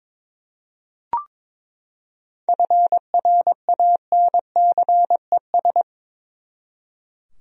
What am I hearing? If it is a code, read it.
Morse code, heard as a keyed tone, about 22 wpm, 700 Hz. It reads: FRANCES
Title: USA Common Names - 22wpm